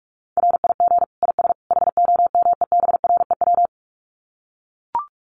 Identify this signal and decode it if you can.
Morse code, heard as a keyed tone, about 44 wpm, 700 Hz. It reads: RIG IS HOMEBREW
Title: Common QSO Phrases 45wpm